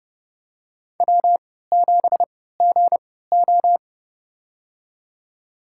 Morse code, heard as a keyed tone, about 32 wpm, spaced as for 17 wpm, 700 Hz. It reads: W7GO
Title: USA Call Signs - 4 Characters - 3x Character Spacing - 30wpm